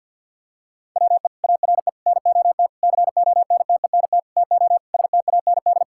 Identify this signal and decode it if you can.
Morse code, heard as a keyed tone, about 52 wpm, 700 Hz. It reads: WE ARE NOT CONTENT TO STAND
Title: Sentences from Top 1000 Words - Review Part 6 - 50wpm